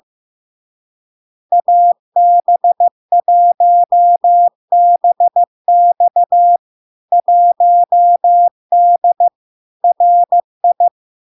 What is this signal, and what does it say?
Morse code, heard as a keyed tone, about 15 wpm, 700 Hz. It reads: AB1BX 1D RI
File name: LARK-FD-2022-Log-in-CW-15wpm-700-Hz